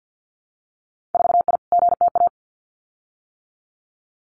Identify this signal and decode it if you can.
Morse code, heard as a keyed tone, about 50 wpm, 700 Hz. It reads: 4I GTA
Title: Winter Field Day Exchanges Encore 50wpm